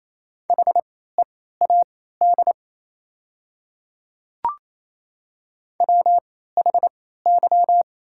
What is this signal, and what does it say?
Morse code, heard as a keyed tone, about 27 wpm, spaced as for 16 wpm, 700 Hz. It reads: HEAD WHY
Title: 3x Character Spacing - Top 500 Words - 28wpm